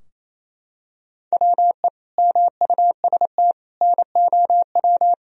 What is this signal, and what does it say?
Morse code, heard as a keyed tone, about 28 wpm, 700 Hz. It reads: WE MUST NOW
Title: Sentences from Top 1000 Words - Review Part 4 - 28wpm